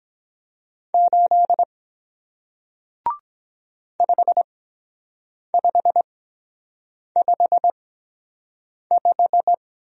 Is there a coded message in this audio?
Yes: Morse code from 0.9 to 9.5 s, heard as a keyed tone, about 23 wpm, 700 Hz. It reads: 8 5 5 5 5